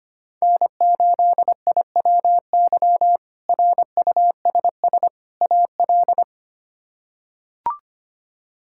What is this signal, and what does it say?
Morse code, heard as a keyed tone, about 25 wpm, 700 Hz. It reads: N8IWY RUSS AL
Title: K1USN SST QSO Practice - 25wpm